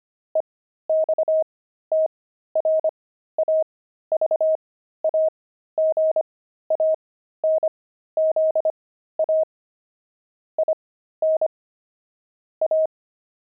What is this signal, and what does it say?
Morse code, heard as a keyed tone, about 25 wpm, spaced as for 13 wpm, 630 Hz. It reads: EXTRAVAGANZA IN A
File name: pangram-fs-13-cs-25-jelly